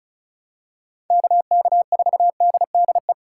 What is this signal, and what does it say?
Morse code, heard as a keyed tone, about 34 wpm, 700 Hz. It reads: KK4DDE
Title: USA Call Signs - 6 Characters - 4x Character Spacing - 35wpm